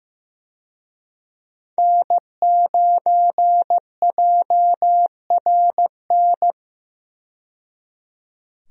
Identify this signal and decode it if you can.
Morse code, heard as a keyed tone, about 15 wpm, 700 Hz. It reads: N9JRN
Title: USA Call Signs - 5 Characters - 15wpm